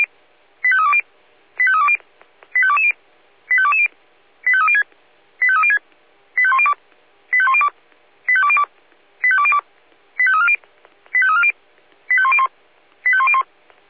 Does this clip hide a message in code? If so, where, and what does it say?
Yes: five-tone selective call (ZVEI) from 0.6 to 13.4 s, heard as quick runs of five tones, tones of about 80 ms. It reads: 85320 85320 85220 85220 85306 85306 85102 85102 85202 85202 85340 85340 85101 85101